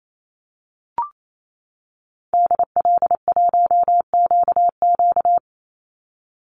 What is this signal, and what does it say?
Morse code, heard as a keyed tone, about 28 wpm, 700 Hz. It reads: DL1QQ
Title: Call Signs - 28wpm